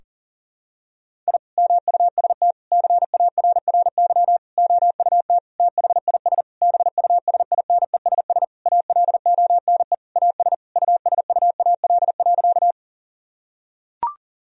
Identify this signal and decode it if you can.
Morse code, heard as a keyed tone, about 40 wpm, 700 Hz. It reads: I MUST CARRY OUT THIS BUSINESS ALONE AS USUAL.
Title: Sentences from Top 1000 Words - Review Part 2 - 40wpm